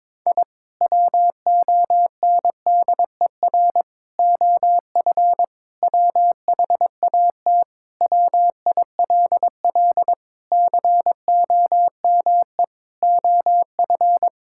Morse code, heard as a keyed tone, about 22 wpm, 700 Hz. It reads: I WONDER OF WHAT WILL COME OF
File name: practice / 22wpm / Sentences with 2-Word Phrases - 22wpm